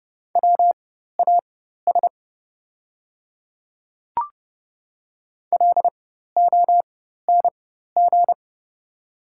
Morse code, heard as a keyed tone, about 30 wpm, spaced as for 14 wpm, 690 Hz. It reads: WAS LONG
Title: 4x Character Spacing - Top 100 Words - 30wpm